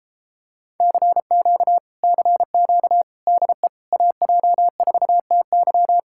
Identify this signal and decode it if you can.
Morse code, heard as a keyed tone, about 32 wpm, 700 Hz. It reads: CQ CQ DE AJ4TY